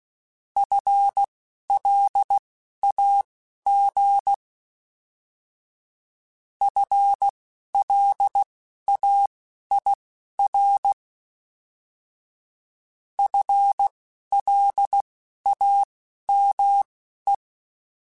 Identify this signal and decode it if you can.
Morse code, heard as a keyed tone, about 16 wpm, spaced as for 11 wpm, 800 Hz. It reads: FLAG FLAIR FLAME